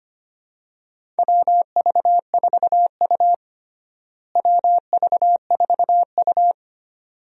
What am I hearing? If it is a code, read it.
Morse code, heard as a keyed tone, about 25 wpm, 700 Hz. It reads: WV4U WV4U
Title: USA Call Signs - 4 Characters - Repeated 2x - 25wpm